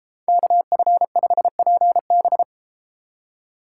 Morse code, heard as a keyed tone, about 31 wpm, 700 Hz. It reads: KF5PB